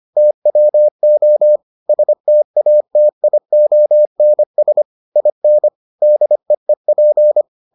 Morse code, heard as a keyed tone, about 25 wpm, 600 Hz. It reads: TWO STATIONS IN DEEP